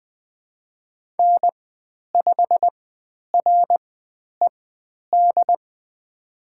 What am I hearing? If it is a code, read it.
Morse code, heard as a keyed tone, about 20 wpm, spaced as for 10 wpm, 700 Hz. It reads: N5RED
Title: 102 Lesson 17g - Letter u - Call Signs 20wpm